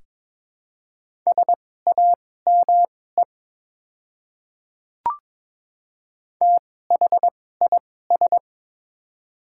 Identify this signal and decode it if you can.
Morse code, heard as a keyed tone, about 22 wpm, spaced as for 16 wpm, 700 Hz. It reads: SAME THIS
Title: 2x Character Spacing - Top 300 Words - 22wpm